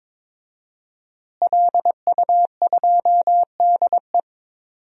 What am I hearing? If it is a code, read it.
Morse code, heard as a keyed tone, about 22 wpm, 700 Hz. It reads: LU2DE
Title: DXCC-100 Call Signs - 5 Characters - Repeated 3x - 22wpm